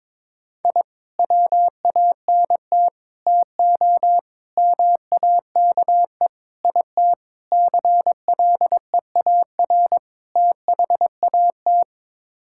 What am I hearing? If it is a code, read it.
Morse code, heard as a keyed tone, about 22 wpm, 700 Hz. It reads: I WANT TO MAKE IT CLEAR THAT
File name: practice / 22wpm / Sets of 7 Words - 22wpm